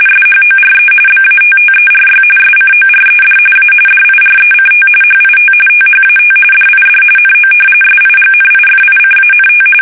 RTTY, about 45.45 baud. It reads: SHORE BATTERY SCORED A DIRECT HIT THAT DID NO SIGNIFICANT
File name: fsk850 / battleshipnj